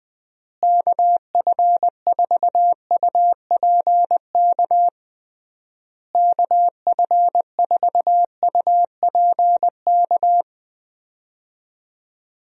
Morse code, heard as a keyed tone, about 20 wpm, 700 Hz. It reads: KF4UPK KF4UPK
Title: USA Call Signs - 6 Characters - Repeated 3x - 20wpm